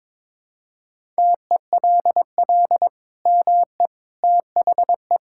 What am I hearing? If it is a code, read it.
Morse code, heard as a keyed tone, about 22 wpm, 700 Hz. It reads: TELL ME THE
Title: Sentences from Top 1000 Words - Review Part 5 - 22wpm